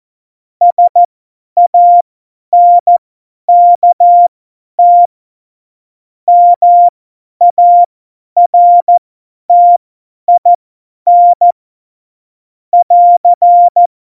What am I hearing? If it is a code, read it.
Morse code, heard as a keyed tone, about 14 wpm, spaced as for 10 wpm, 700 Hz. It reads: SANKT MARTIN <AR>
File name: Morsekurs / Witze / Witz 21. August 2018, 50 F70